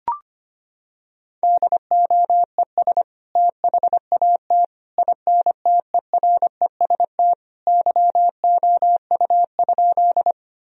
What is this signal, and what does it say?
Morse code, heard as a keyed tone, about 25 wpm, 700 Hz. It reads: DOES THAT INTEREST YOU?
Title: Sentences from Top 1000 Words - Review Part 2 - 25wpm